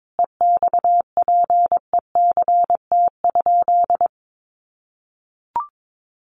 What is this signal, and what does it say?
Morse code, heard as a keyed tone, about 22 wpm, 700 Hz. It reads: EXPECT?
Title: Sentences from Top 1100 Words 22wpm